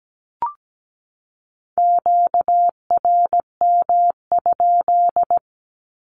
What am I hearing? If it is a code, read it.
Morse code, heard as a keyed tone, about 17 wpm, 700 Hz. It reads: QRM?
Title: Q-Codes 17wpm